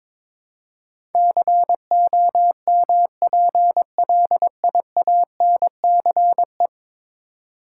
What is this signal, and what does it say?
Morse code, heard as a keyed tone, about 22 wpm, 700 Hz. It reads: COMPLIANCE
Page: morse-code-ninja-practice-22wpm